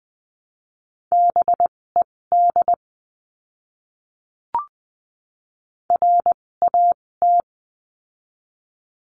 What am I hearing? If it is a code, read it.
Morse code, heard as a keyed tone, about 20 wpm, spaced as for 16 wpm, 700 Hz. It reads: BED RAT
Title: 256 Lesson 42a - Words 12-20wpm